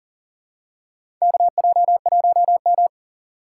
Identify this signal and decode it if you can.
Morse code, heard as a keyed tone, about 40 wpm, 700 Hz. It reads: KJ1M
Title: USA Call Signs - 2x Character Spacing - 40wpm